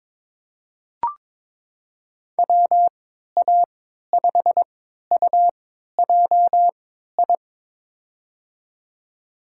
Morse code, heard as a keyed tone, about 22 wpm, spaced as for 12 wpm, 700 Hz. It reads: WA5UJI
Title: USA Call Signs - 6 Characters - 3x Character Spacing - 22wpm